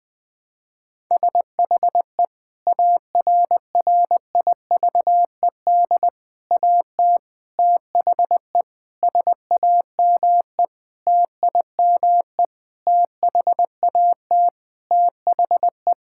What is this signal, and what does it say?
Morse code, heard as a keyed tone, about 20 wpm, 700 Hz. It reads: SHE ARRIVED AT THE SAME TIME THAT THE
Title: Sentences with 5-Word Phrases - 20wpm